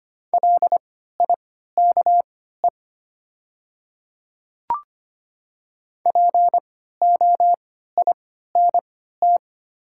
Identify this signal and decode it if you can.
Morse code, heard as a keyed tone, about 25 wpm, spaced as for 14 wpm, 700 Hz. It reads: LIKE POINT
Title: 3x Character Spacing - Top 200 Words - 25wpm